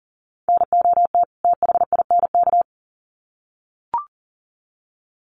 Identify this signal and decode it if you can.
Morse code, heard as a keyed tone, about 40 wpm, 700 Hz. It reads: NOT THINK
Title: Sets of 2 Words 40wpm